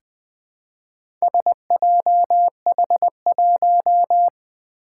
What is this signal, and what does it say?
Morse code, heard as a keyed tone, about 20 wpm, 700 Hz. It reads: SJH1
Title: Four Challenging Letter-Number Combinations 20wpm